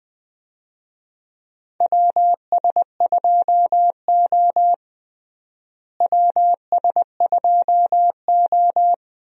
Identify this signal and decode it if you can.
Morse code, heard as a keyed tone, about 20 wpm, 700 Hz. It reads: WS2O WS2O
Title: USA Call Signs - 4 Characters - Repeated 3x - 20wpm